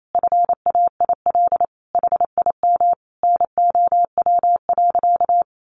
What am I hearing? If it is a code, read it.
Morse code, heard as a keyed tone, about 28 wpm, 700 Hz. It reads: FAIL HIM NOW.
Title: Sentences from Top 1100 Words 28wpm